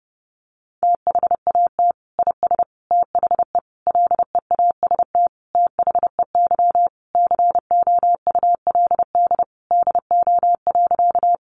Morse code, heard as a keyed tone, about 30 wpm, 700 Hz. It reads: THAT IS THE LEAST THEY COULD DO.
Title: Sentences from Top 700 Words 30wpm